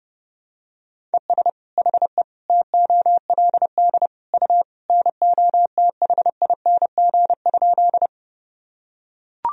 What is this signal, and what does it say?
Morse code, heard as a keyed tone, about 30 wpm, 700 Hz. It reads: ES HE TOLD U NOTHING?
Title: Sentences from Top 500 Words and Abbreviations 30wpm